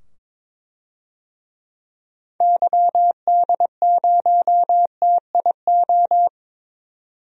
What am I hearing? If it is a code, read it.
Morse code, heard as a keyed tone, about 22 wpm, 700 Hz. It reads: YD0TIO
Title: DXCC-100 Call Signs - 6 Characters - 22wpm